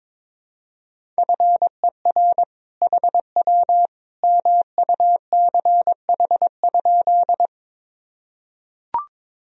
Morse code, heard as a keyed tone, about 22 wpm, 700 Hz. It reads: FER HW MUCH?
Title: Sentences from Top 200 Words and Abbreviations 22wpm